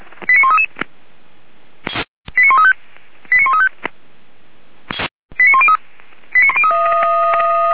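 Five-tone selective call (ZVEI), heard as quick runs of five tones, tones of about 80 ms. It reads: 89133 89135 89135 89103
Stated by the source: ZVEI five-tone call received on SDR.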